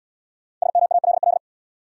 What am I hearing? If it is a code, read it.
Morse code, heard as a keyed tone, about 75 wpm, 700 Hz. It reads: SMALL